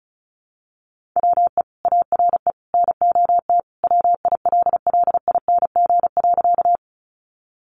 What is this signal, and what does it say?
Morse code, heard as a keyed tone, about 34 wpm, 700 Hz. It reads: WE ARE NOT WILLING.
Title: Sentences from Top 1000 Words 35wpm